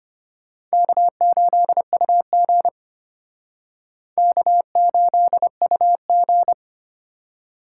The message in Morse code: K8UG K8UG

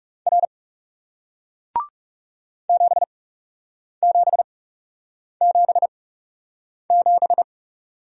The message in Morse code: R 7777

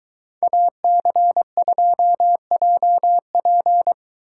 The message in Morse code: AC2JP

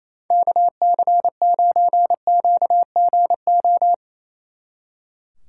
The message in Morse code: KC9QGO

